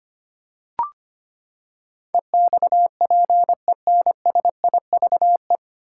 Morse code, heard as a keyed tone, about 25 wpm, 700 Hz. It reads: EXPENSIVE
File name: Nine Letter Words - 25wpm